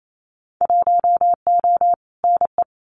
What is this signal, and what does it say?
Morse code, heard as a keyed tone, about 28 wpm, 700 Hz. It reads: 1O NE
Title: Winter Field Day Exchanges 28wpm